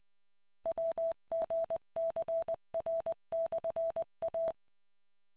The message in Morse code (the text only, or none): WGCR/A